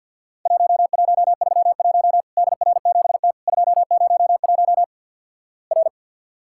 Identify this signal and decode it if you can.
Morse code, heard as a keyed tone, about 50 wpm, 700 Hz. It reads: 1131 DR7T 201 R
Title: Worked All Europe DX Contest - QTC Exchange - 50wpm